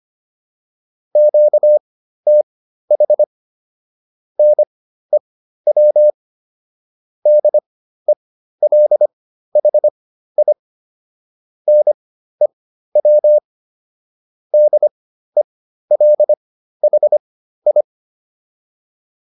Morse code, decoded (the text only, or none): QTH NEW DELHI NEW DELHI